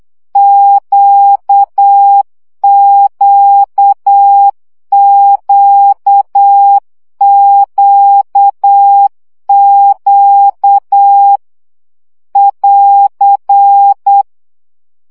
Morse code, heard as a keyed tone, about 8 wpm, 800 Hz. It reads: QQQQQ <AR>
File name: CW gyakorlas q 7 baud